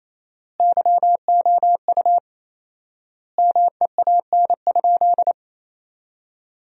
Morse code, heard as a keyed tone, about 27 wpm, 700 Hz. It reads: YOU MEAN?
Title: 4x Word-Spacing - Sentences from Top 200 Words - 28wpm